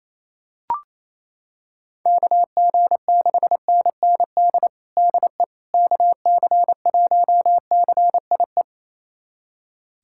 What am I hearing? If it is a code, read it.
Morse code, heard as a keyed tone, about 28 wpm, 700 Hz. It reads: KG6NND DE KC1CIE
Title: USA Callsign DE Callsign - 6 Characters - 28wpm